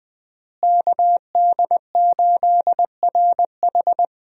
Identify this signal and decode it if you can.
Morse code, heard as a keyed tone, about 20 wpm, 700 Hz. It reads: KD8RH
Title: USA Call Signs - 5 Characters - Repeated 2x - 20wpm